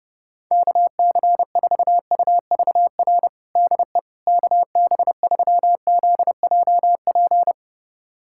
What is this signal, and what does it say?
Morse code, heard as a keyed tone, about 30 wpm, 700 Hz. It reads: KC4UVR DE KB3ZJP
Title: USA Callsign DE Callsign 30wpm